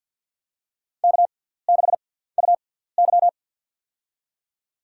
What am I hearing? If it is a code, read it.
Morse code, heard as a keyed tone, about 52 wpm, spaced as for 17 wpm, 700 Hz. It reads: K6UY
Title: USA Call Signs - 4 Characters - 6x Character Spacing - 50wpm